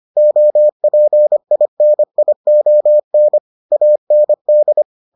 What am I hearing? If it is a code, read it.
Morse code, heard as a keyed tone, about 25 wpm, 600 Hz. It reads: OPINION AND